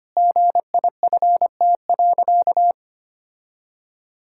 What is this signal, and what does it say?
Morse code, heard as a keyed tone, about 25 wpm, 700 Hz. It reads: GIFT.
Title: Sentences from Top 1000 Words - Review Part 6 - 25wpm